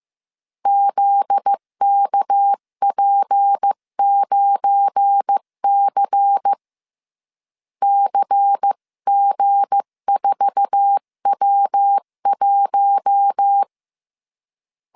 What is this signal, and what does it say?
Morse code, heard as a keyed tone, about 15 wpm, 800 Hz. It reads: ZKP9C CG4W1
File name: CW / DET / 1 9 1500C S B 14wpm 3302